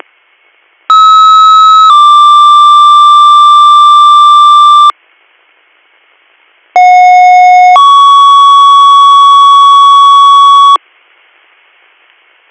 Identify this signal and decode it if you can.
Two-tone page, heard as a one-second tone, then a three-second tone, about 1290 then 1180 Hz, about 730 then 1120 Hz. It is heard from 0.9 to 10.8 s.